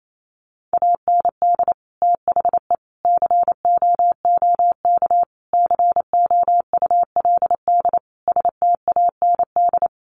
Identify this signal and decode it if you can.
Morse code, heard as a keyed tone, about 28 wpm, 700 Hz. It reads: AND THE COOK COULD STAND
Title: Sentences from Top 1300 Words 28wpm